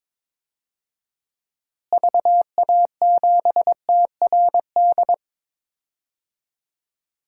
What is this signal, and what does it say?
Morse code, heard as a keyed tone, about 21 wpm, 700 Hz. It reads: VA7TRD
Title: DXCC-100 Call Signs - 6 Characters - Repeated 2x - 22wpm